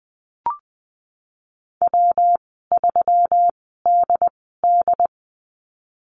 Morse code, heard as a keyed tone, about 20 wpm, spaced as for 14 wpm, 700 Hz. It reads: W3DD
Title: 128 Lesson 21g - Number 6 - Call Signs 20wpm